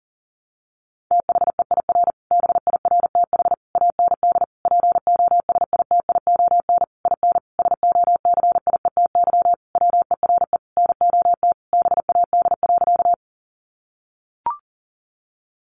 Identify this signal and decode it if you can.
Morse code, heard as a keyed tone, about 40 wpm, 700 Hz. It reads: THEIR BIRTH AND POSITION IN SOCIETY WERE NOT BAD.
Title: Sentences from Top 900 Words 40wpm